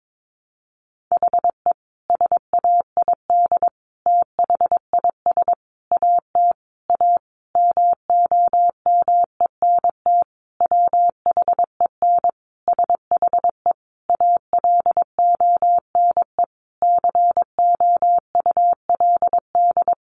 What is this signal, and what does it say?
Morse code, heard as a keyed tone, about 22 wpm, 700 Hz. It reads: HE SAID THIS AT A MOMENT WHEN SHE ALONE COULD
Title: Sentences from Top 1000 Words 22wpm